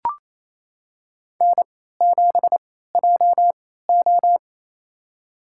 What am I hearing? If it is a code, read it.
Morse code, heard as a keyed tone, about 28 wpm, spaced as for 16 wpm, 700 Hz. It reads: N7JO
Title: USA Call Signs - 4 Characters - 3x Character Spacing - 28wpm